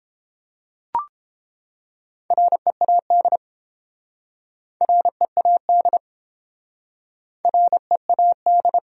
Morse code, heard as a keyed tone, about 30 wpm, 700 Hz. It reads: READ READ READ